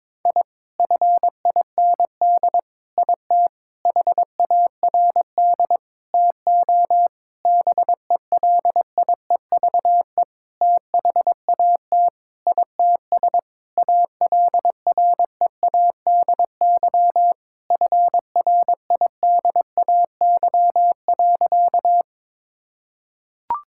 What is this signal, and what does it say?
Morse code, heard as a keyed tone, about 22 wpm, 700 Hz. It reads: I FIND IT HARD TO BELIEVE THAT ITS ALREADY FRIDAY.
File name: Sentences with 7-Word Phrases - 22wpm